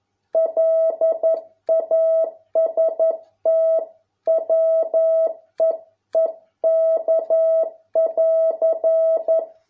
Morse code, heard as a keyed tone, about 11 wpm, 650 Hz. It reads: LAST WEEK<AR>